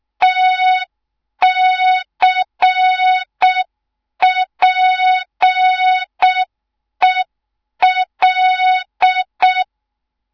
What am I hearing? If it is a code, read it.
Morse code, heard as a keyed tone, about 6 wpm, 3700 Hz. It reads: TCPEL
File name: Morze-HA2MN / hang / 24 GYAKORLAT